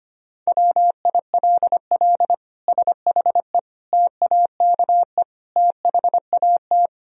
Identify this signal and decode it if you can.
Morse code, heard as a keyed tone, about 25 wpm, 700 Hz. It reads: WILL SHE TAKE THAT
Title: Sentences from Top 1000 Words - Review Part 4 - 25wpm